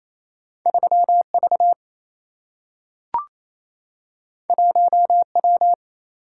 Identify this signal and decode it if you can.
Morse code, heard as a keyed tone, about 28 wpm, 700 Hz. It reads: 3V 1W
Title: Single-Letter Single-Number 28wpm